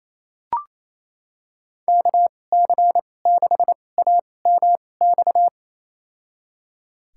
Morse code, heard as a keyed tone, about 28 wpm, spaced as for 20 wpm, 700 Hz. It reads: KC6AMX
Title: USA Call Signs - 2x Character Spacing - 28wpm